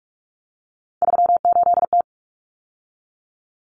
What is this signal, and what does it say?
Morse code, heard as a keyed tone, about 44 wpm, 700 Hz. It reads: 38T
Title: Common Cut Numbers - 3-Digit 45wpm